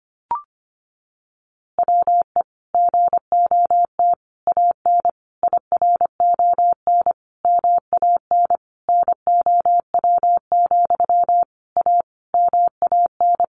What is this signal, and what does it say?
Morse code, heard as a keyed tone, about 25 wpm, 700 Hz. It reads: WE GOT AN IRON MAN NOW, A MAN
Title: A Sound Of Thunder 25wpm